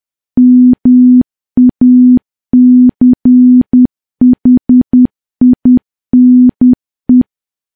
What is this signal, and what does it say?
Morse code, heard as a keyed tone, about 10 wpm, 250 Hz. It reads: MACHINE
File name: morse / morse-training-10WPM